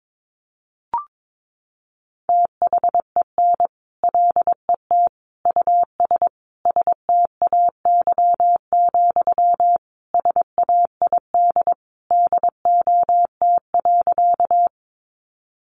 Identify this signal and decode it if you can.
Morse code, heard as a keyed tone, about 22 wpm, 700 Hz. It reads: THEN LET US STAY, SAID DOT.
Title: Sentences from Top 1500 Words 22wpm